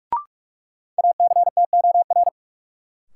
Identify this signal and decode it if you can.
Morse code, heard as a keyed tone, about 46 wpm, 700 Hz. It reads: ACTOR
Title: Sentences from Top 1500 Words - New Word Drill - 45wpm